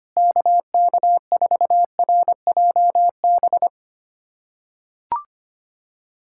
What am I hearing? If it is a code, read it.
Morse code, heard as a keyed tone, about 25 wpm, 700 Hz. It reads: KK4RJB
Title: USA Callsign DE Callsign - 6 Characters - 25wpm